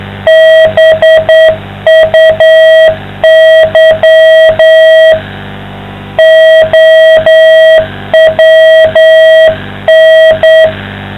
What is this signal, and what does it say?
Morse code, heard as a keyed tone, about 9 wpm, 660 Hz. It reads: BUY OWN